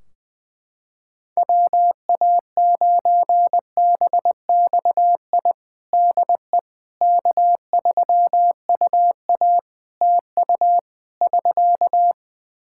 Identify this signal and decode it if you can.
Morse code, heard as a keyed tone, about 20 wpm, 700 Hz. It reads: WA9BXI DE K3UA TU <SK>
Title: K1USN SST QSO Practice 20wpm